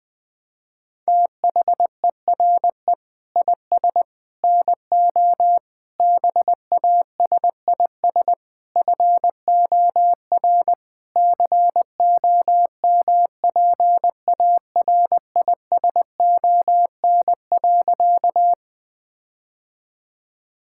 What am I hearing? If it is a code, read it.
Morse code, heard as a keyed tone, about 20 wpm, 700 Hz. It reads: THERE IS NO BASIS FOR COMPARISON.
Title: Sentences from Top 1600 Words 20wpm